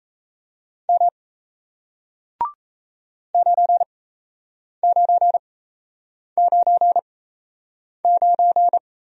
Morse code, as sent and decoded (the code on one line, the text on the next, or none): -- / ----. ----. ----. ----.
M 9999